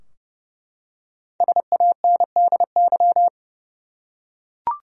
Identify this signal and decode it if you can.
Morse code, heard as a keyed tone, about 30 wpm, 700 Hz. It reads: SANDY
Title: USA Common Names - 30wpm